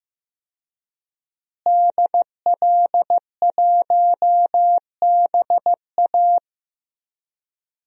Morse code, heard as a keyed tone, about 15 wpm, 700 Hz. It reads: DL1BA
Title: DXCC-100 Call Signs - 5 Characters - Repeated 2x - 15wpm